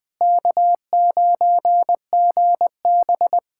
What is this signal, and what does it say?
Morse code, heard as a keyed tone, about 20 wpm, 700 Hz. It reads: K9GB
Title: USA Callsign DE Callsign - 4 Characters - 20wpm